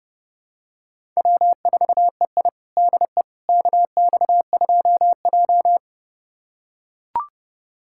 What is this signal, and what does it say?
Morse code, heard as a keyed tone, about 32 wpm, 700 Hz. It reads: W4EI DE KX2J
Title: USA Callsign DE Callsign - 4 Characters - 30wpm